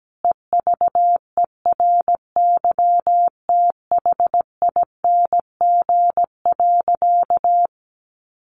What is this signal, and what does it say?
Morse code, heard as a keyed tone, about 17 wpm, 700 Hz. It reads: EVERYTHING.
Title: Sentences from Top 700 Words 17wpm